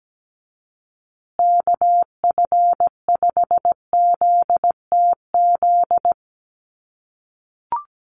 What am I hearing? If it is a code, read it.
Morse code, heard as a keyed tone, about 17 wpm, 700 Hz. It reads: KF5ZTZ
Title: USA Call Signs Repeated 3x 17wpm